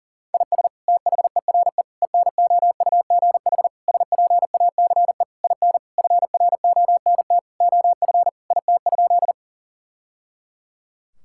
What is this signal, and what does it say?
Morse code, heard as a keyed tone, about 37 wpm, 700 Hz. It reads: IS THERE ENOUGH SPACE IN FRONT OF IT?
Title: Sentences from Top 1000 Words - Review Part 5 - 40wpm